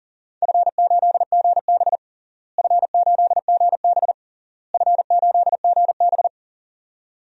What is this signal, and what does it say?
Morse code, heard as a keyed tone, about 40 wpm, 700 Hz. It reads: F8GB F8GB F8GB
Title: DXCC-100 Call Signs - 4 Characters - Repeated 3x - 40wpm